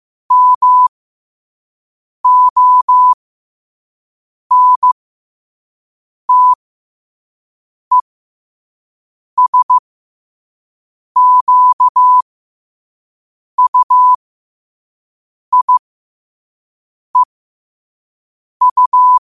Morse code, heard as a keyed tone, about 15 wpm, spaced as for 5 wpm, 1000 Hz. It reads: MONTESQUIEU
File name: qotdfiles / 5wpm